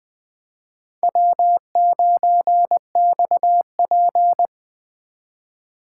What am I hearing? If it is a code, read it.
Morse code, heard as a keyed tone, about 24 wpm, 700 Hz. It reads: W9XP
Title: Speed Racing - USA Call Signs 20wpm